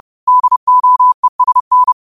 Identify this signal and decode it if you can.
Morse code, heard as a keyed tone, about 30 wpm, 1000 Hz. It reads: DOESN
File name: qotdfiles / 30wpm